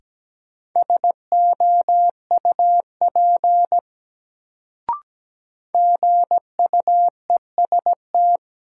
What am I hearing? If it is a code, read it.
Morse code, heard as a keyed tone, about 17 wpm, 700 Hz. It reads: SOUP GUEST
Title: Sentences from Top 1600 Words - New Word Drill Alt - 17wpm